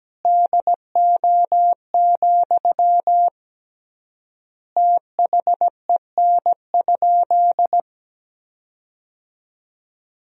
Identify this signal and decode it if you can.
Morse code, heard as a keyed tone, about 17 wpm, 700 Hz. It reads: DO, THEN?
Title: 3x Word-Spacing - Sentences from Top 100 Words - 17wpm